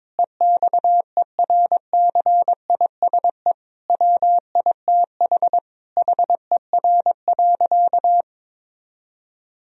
Morse code, heard as a keyed tone, about 22 wpm, 700 Hz. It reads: EXERCISE WITH HER.
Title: Sentences from Top 1000 Words - Review Part 4 - 22wpm